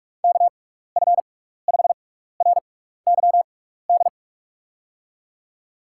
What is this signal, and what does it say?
Morse code, heard as a keyed tone, about 41 wpm, spaced as for 15 wpm, 700 Hz. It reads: KF5RYD
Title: USA Call Signs - 6 Characters - 6x Character Spacing - 45wpm